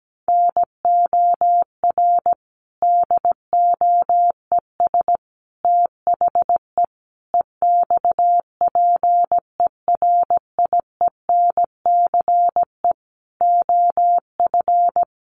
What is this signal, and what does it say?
Morse code, heard as a keyed tone, about 17 wpm, 700 Hz. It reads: NOR DOES THE EXPERIENCE OF
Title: Sentences from Top 1500 Words 17wpm